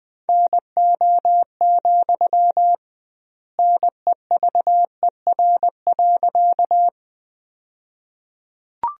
Morse code, heard as a keyed tone, about 20 wpm, 700 Hz. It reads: NO, NEVER.
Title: 2x Word-Spacing - Sentences from Top 300 Words - 20wpm